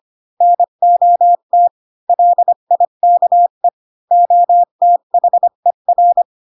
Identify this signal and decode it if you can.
Morse code, heard as a keyed tone, about 25 wpm, 700 Hz. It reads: NOT LIKE OTHER
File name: Pinocchio / Pinocchio ch8 20 25 ik1hge